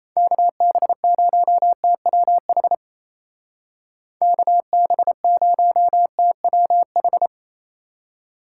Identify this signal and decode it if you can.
Morse code, heard as a keyed tone, about 37 wpm, 700 Hz. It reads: KB0TWH KB0TWH